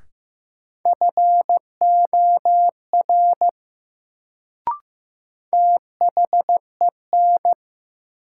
Morse code, heard as a keyed tone, about 15 wpm, 700 Hz. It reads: FOR THEN
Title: Sentences from Top 100 Words - New Word Drill - 15wpm